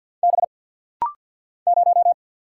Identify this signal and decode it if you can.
Morse code, heard as a keyed tone, about 50 wpm, 700 Hz. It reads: B 0
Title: Single Letter-Number - Rapid-Fire 50wpm